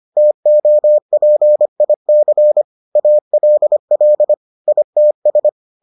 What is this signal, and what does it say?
Morse code, heard as a keyed tone, about 25 wpm, 600 Hz. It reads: TOPIC ALL ITS